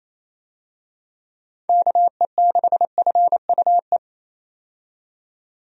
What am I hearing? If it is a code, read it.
Morse code, heard as a keyed tone, about 27 wpm, 700 Hz. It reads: KE6FUE
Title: USA Call Signs - 6 Characters - 28wpm